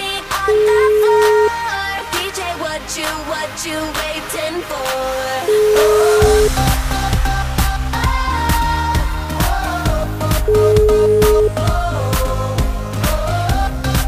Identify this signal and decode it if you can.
Ringback tone (Europe, 425 Hz), with noise.